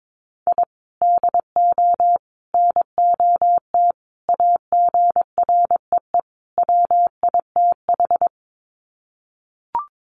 Morse code, heard as a keyed tone, about 22 wpm, 700 Hz. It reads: I DO NOT AGREE WITH
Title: Sets of 5 Words Encore 22wpm